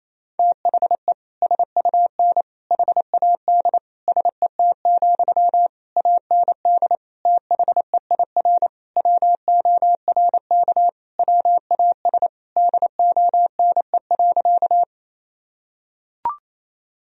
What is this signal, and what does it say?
Morse code, heard as a keyed tone, about 28 wpm, 700 Hz. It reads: THE SUN HAD SET, AND THEIR WORK WAS DONE.
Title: Sentences from Top 1000 Words - Review Part 1 - 28wpm